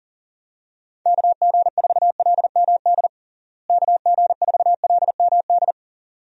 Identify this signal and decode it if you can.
Morse code, heard as a keyed tone, about 40 wpm, 700 Hz. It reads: KG4LMD KG4LMD
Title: USA Call Signs - 6 Characters - Repeated 2x - 40wpm